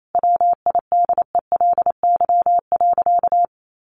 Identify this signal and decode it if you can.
Morse code, heard as a keyed tone, about 28 wpm, 700 Hz. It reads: WIDELY.
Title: Sentences from Top 1200 Words 28wpm